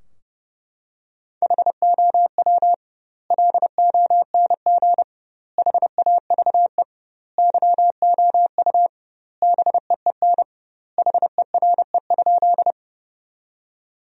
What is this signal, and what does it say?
Morse code, heard as a keyed tone, about 30 wpm, 700 Hz. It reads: HOW LONG HAVE YOU BEEN HERE?
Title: 2x Word-Spacing - Sentences from Top 200 Words - 30wpm